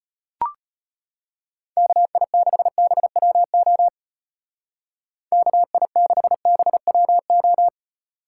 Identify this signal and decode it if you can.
Morse code, heard as a keyed tone, about 34 wpm, 700 Hz. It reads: KI6BWO KI6BWO